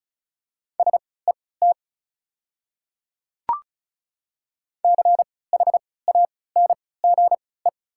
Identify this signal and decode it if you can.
Morse code, heard as a keyed tone, about 36 wpm, spaced as for 20 wpm, 700 Hz. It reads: SET CHANGE
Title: 3x Character Spacing - Top 200 Words - 35wpm